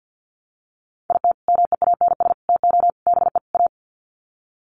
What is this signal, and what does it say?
Morse code, heard as a keyed tone, about 50 wpm, 700 Hz. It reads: IT MEANS TO BE A
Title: Sets of 5 Words Encore2 50wpm